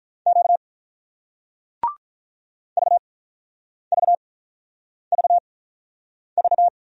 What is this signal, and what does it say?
Morse code, heard as a keyed tone, about 48 wpm, spaced as for 9 wpm, 700 Hz. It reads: Q VVVV